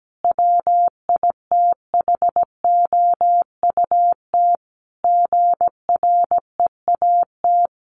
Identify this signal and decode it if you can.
Morse code, heard as a keyed tone, about 17 wpm, 700 Hz. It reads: WITHOUT GREAT
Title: Sentences from Top 1400 Words 17wpm